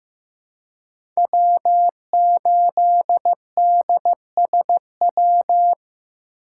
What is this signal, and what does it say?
Morse code, heard as a keyed tone, about 15 wpm, 700 Hz. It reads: W8DSW
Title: USA Call Signs - 2x Character Spacing - 15wpm